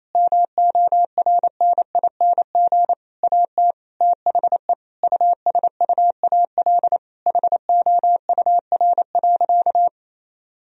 Morse code, heard as a keyed tone, about 28 wpm, 700 Hz. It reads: MORNING AT THE USUAL HOUR.
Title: Sentences from Top 1000 Words - Review Part 2 - 28wpm